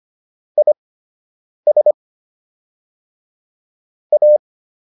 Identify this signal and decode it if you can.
Morse code, heard as a keyed tone, about 25 wpm, spaced as for 8 wpm, 600 Hz. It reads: IS A